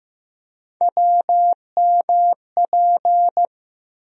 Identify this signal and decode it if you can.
Morse code, heard as a keyed tone, about 15 wpm, 700 Hz. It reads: WMP
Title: USA Call Sign Postfixes - 3 Letters 15wpm